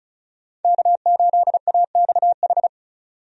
Morse code, heard as a keyed tone, about 34 wpm, 700 Hz. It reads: K8AXH
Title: USA Call Signs - 5 Characters - 35wpm